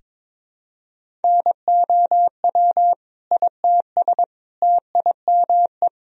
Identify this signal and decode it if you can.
Morse code, heard as a keyed tone, about 22 wpm, 700 Hz. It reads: NOW ITS TIME
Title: Sentences from Top 1000 Words - Review Part 1 - 22wpm